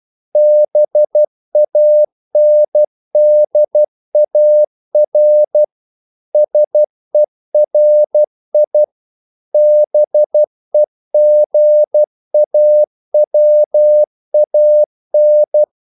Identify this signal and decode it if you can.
Morse code, heard as a keyed tone, about 12 wpm, 600 Hz. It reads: BANDAR SERI BEGAWAN
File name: CW / Practice / NationalCapitals12